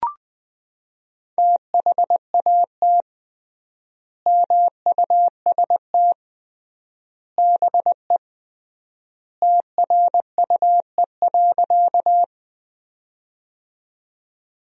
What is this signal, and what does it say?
Morse code, heard as a keyed tone, about 20 wpm, 700 Hz. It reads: THAT MUST BE TRUE.
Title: 3x Word-Spacing - Sentences from Top 400 Words - 20wpm